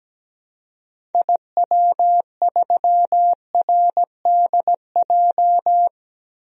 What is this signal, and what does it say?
Morse code, heard as a keyed tone, about 17 wpm, 700 Hz. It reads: IW3RDJ
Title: DXCC-100 Call Signs - 6 Characters - Repeated 2x - 17wpm